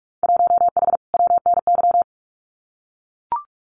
Morse code, heard as a keyed tone, about 44 wpm, 700 Hz. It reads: 1H WNY
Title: Winter Field Day Exchanges 45wpm